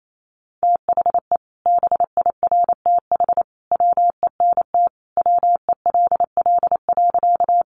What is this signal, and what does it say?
Morse code, heard as a keyed tone, about 28 wpm, 700 Hz. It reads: THE BIRTH WENT WELL.
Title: Sentences from Top 800 Words 28wpm